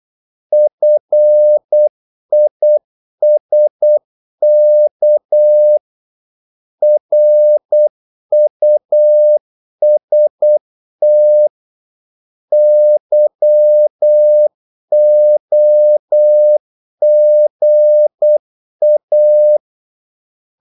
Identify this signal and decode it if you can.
Morse code, heard as a keyed tone, about 8 wpm, 600 Hz. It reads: FISK RUST YOGA
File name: Korte ord DK 8wpm